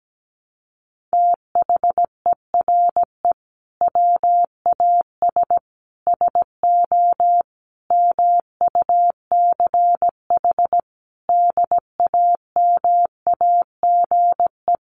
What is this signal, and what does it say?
Morse code, heard as a keyed tone, about 17 wpm, 700 Hz. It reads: THERE WAS SO MUCH DAMAGE
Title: Sentences from Top 1100 Words 17wpm